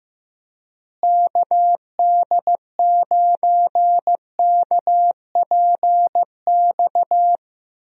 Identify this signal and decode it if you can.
Morse code, heard as a keyed tone, about 15 wpm, 700 Hz. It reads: KD9KPX
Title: USA Call Signs - 6 Characters - Repeated 3x - 15wpm